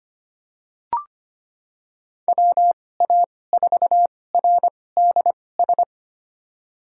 Morse code, heard as a keyed tone, about 25 wpm, spaced as for 18 wpm, 700 Hz. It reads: WA4RDS
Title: USA Call Signs - 2x Character Spacing - 25wpm